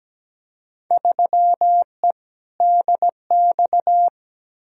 Morse code, heard as a keyed tone, about 17 wpm, 700 Hz. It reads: 3E DX
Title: Field Day Exchanges Encore - 17wpm